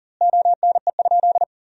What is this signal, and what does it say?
Morse code, heard as a keyed tone, about 40 wpm, 700 Hz. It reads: ONE?
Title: 2x Word-Spacing - Sentences from Top 300 Words - 40wpm